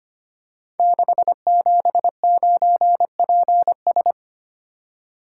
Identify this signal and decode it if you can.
Morse code, heard as a keyed tone, about 25 wpm, 700 Hz. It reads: 679PS
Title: Code Groups - 5 Letter:Numbers - 25wpm